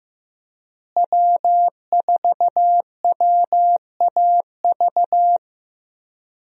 Morse code, heard as a keyed tone, about 15 wpm, 700 Hz. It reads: W4WAV
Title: USA Call Signs - 5 Characters - Repeated 2x - 15wpm